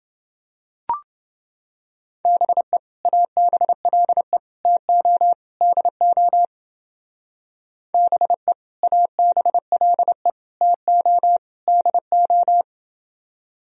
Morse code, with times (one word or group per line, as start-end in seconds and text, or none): 2.2-2.8 BE
3.0-4.4 ABLE
4.6-5.3 TO
5.6-6.5 DO
7.9-8.5 BE
8.8-10.3 ABLE
10.6-11.4 TO
11.7-12.6 DO